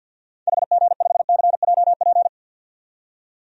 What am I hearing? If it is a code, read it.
Morse code, heard as a keyed tone, about 52 wpm, 700 Hz. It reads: HG5CJP
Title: DXCC-100 Call Signs - 6 Characters - 50wpm